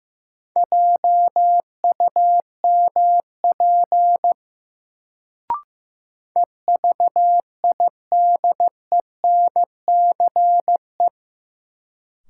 Morse code, heard as a keyed tone, about 16 wpm, 700 Hz. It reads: JUMP EVIDENCE
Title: Sentences from Top 1300 Words - New Word Drill - 15wpm